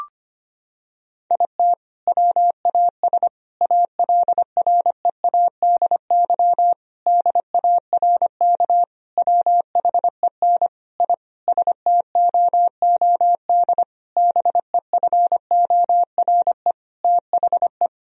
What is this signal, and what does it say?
Morse code, heard as a keyed tone, about 25 wpm, 700 Hz. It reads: IT WAS ALREADY DARK WHEN I STOOD BEFORE THE
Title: Sentences from Top 1000 Words - Review Part 2 - 25wpm